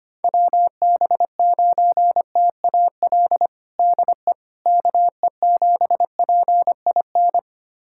Morse code, heard as a keyed tone, about 25 wpm, 700 Hz. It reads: WB9TAL DE KE7PIN
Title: USA Callsign DE Callsign - 6 Characters - 25wpm